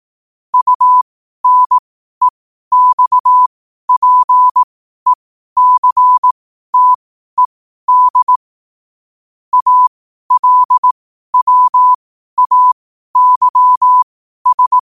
Morse code, heard as a keyed tone, about 18 wpm, spaced as for 12 wpm, 1000 Hz. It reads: UNEXPECTED ALWAYS
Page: New quotes every day in morse code at 12 Words per minute.